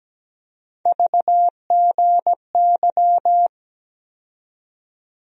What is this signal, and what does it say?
Morse code, heard as a keyed tone, about 17 wpm, 700 Hz. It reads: VGY
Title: USA Call Sign Postfixes - 3 Letters 17wpm